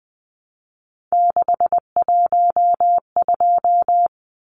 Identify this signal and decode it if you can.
Morse code, heard as a keyed tone, about 20 wpm, 700 Hz. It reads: 612
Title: Numbers - 3-Digit 20wpm